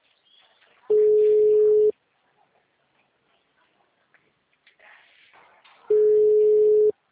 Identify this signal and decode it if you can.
Ringback tone (Europe, 425 Hz).